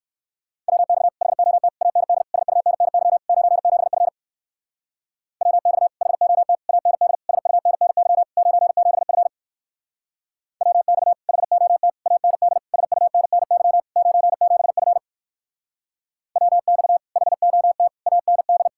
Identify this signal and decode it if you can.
Morse code, heard as a keyed tone, about 64 wpm, 700 Hz. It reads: WX HOT AND SUNNY 97F WX HOT AND SUNNY 97F WX HOT AND SUNNY 97F WX HOT AND